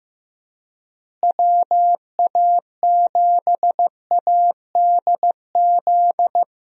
Morse code, heard as a keyed tone, about 15 wpm, 700 Hz. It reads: WA7ADZ
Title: USA Call Signs - 6 Characters - Repeated 2x - 15wpm